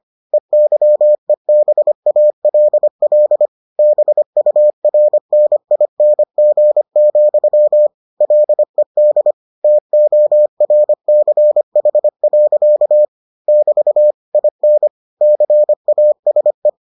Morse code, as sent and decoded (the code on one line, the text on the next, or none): . -.-- . -... .- .-.. .-.. / -... ..- .-. -. .. -. --. --..-- / .-.. -.. / - --- .-. -.-. .... .-.-.- / -...- / .. -. / -.-. .- ... .
EYEBALL BURNING, LED TORCH. <BT> IN CASE